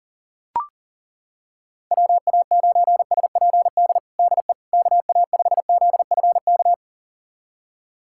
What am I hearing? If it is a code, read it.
Morse code, heard as a keyed tone, about 37 wpm, 700 Hz. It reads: WA9SPD DE KA5ZFK
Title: USA Callsign DE Callsign - 6 Characters - 40wpm